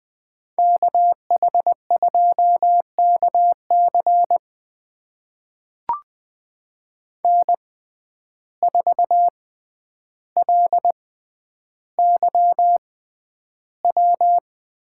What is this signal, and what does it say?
Morse code, heard as a keyed tone, about 20 wpm, spaced as for 10 wpm, 700 Hz. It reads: KH2KC N4LYW
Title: USA Call Signs - 5 Characters - 6x Character Spacing - 20wpm